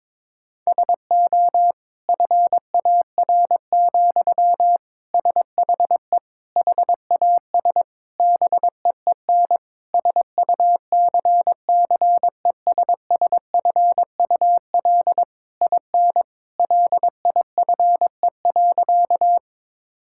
Morse code, heard as a keyed tone, about 22 wpm, 700 Hz. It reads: SO FAR, SHE HAS BEEN SUCCESSFUL IN LIFE.
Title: Sentences from Top 1000 Words - Review Part 6 - 22wpm